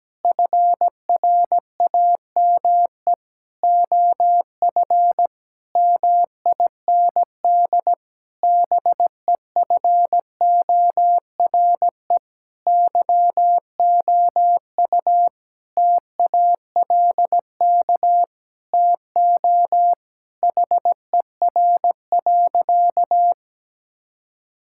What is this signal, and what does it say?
Morse code, heard as a keyed tone, about 17 wpm, 700 Hz. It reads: FRAME OF MIND BEFORE YOU TALK TO HER.
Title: Sentences from Top 1000 Words - Review Part 6 - 17wpm